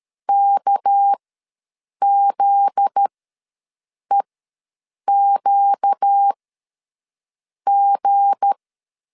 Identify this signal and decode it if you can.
Morse code, heard as a keyed tone, about 13 wpm, spaced as for 6 wpm, 800 Hz. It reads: KZEQG
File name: LEZIONI / 7 GF9Z / 7 GF9Z ws 12wpm 3321